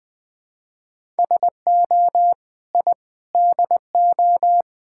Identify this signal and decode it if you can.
Morse code, heard as a keyed tone, about 20 wpm, 690 Hz. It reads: SO I DO
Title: Build A Phrase - 4-Word Phrases - 20wpm